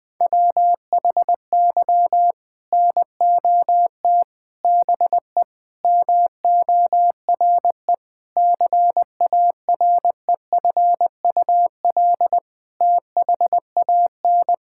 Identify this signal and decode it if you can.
Morse code, heard as a keyed tone, about 20 wpm, 700 Hz. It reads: WHY NOT BE MORE CAREFUL THAN
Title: Sentences from Top 1600 Words 20wpm